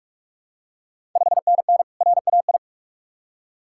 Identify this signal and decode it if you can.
Morse code, heard as a keyed tone, about 44 wpm, 690 Hz. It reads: 5NN RAI
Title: Ontario QSO Party Exchange 45wpm